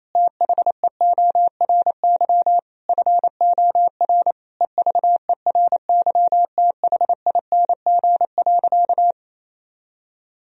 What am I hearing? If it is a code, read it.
Morse code, heard as a keyed tone, about 28 wpm, 700 Hz. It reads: THEORY FOR EVERYTHING.
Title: Sentences from Top 1000 Words - Review Part 3 - 28wpm